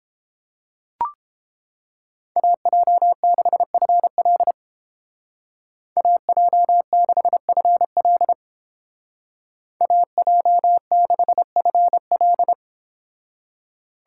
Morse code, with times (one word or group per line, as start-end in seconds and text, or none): 2.4-4.5 AJ6FL
6.0-8.3 AJ6FL
9.8-12.5 AJ6FL